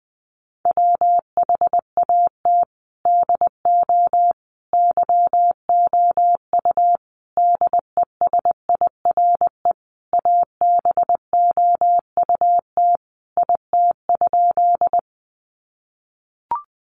Morse code, heard as a keyed tone, about 19 wpm, 700 Hz. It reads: WHAT DO YOU DESIRE ABOUT IT?
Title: Sentences from Top 1100 Words 20wpm